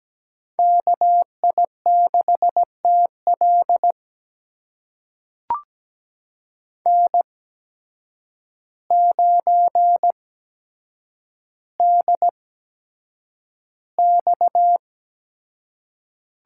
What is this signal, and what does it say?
Morse code, heard as a keyed tone, about 17 wpm, spaced as for 8 wpm, 700 Hz. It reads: KI6TL N9DX
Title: USA Call Signs - 5 Characters - 8x Character Spacing - 17wpm